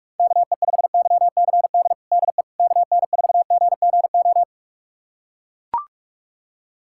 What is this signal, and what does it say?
Morse code, heard as a keyed tone, about 42 wpm, 700 Hz. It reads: KE5YCD DE KN4GGO
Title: USA Callsign DE Callsign - 6 Characters - 45wpm